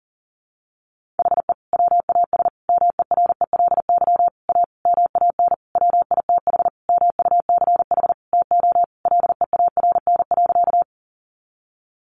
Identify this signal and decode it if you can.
Morse code, heard as a keyed tone, about 40 wpm, 700 Hz. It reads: HE WAS MERELY A MAN WITH MUCH TO LEARN.
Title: Sentences from Top 1400 Words 40wpm